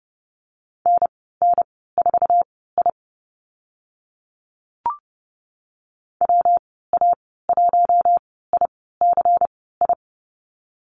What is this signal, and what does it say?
Morse code, heard as a keyed tone, about 30 wpm, spaced as for 17 wpm, 700 Hz. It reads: NN4I WA1ICI
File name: Lesson 19g - Letter w - Call Signs 30wpm